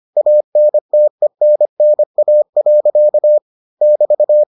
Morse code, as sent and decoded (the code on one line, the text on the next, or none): .- -. - . -. -. .- .-.-.- / -...-
ANTENNA. <BT>